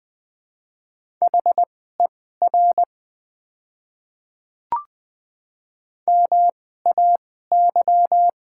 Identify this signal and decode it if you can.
Morse code, heard as a keyed tone, about 20 wpm, spaced as for 15 wpm, 700 Hz. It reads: HER MAY
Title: 2x Character Spacing - Top 100 Words - 20wpm